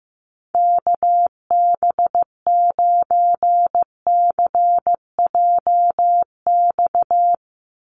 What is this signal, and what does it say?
Morse code, heard as a keyed tone, about 15 wpm, 700 Hz. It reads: KB9CJX
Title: USA Call Signs 15wpm